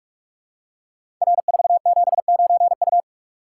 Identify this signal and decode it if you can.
Morse code, heard as a keyed tone, about 44 wpm, 700 Hz. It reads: R479U
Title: Code Groups - 5 Letter:Numbers - 45wpm